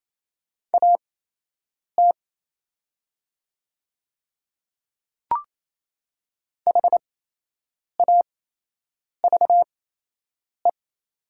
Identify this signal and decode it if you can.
Morse code, heard as a keyed tone, about 27 wpm, spaced as for 8 wpm, 700 Hz. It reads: AT HAVE
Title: 8x Character Spacing - Top 200 Words - 28wpm